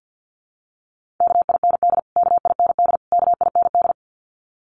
Morse code, heard as a keyed tone, about 50 wpm, 700 Hz. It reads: KIND KIND KIND
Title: Top 300 Words Repeated 50wpm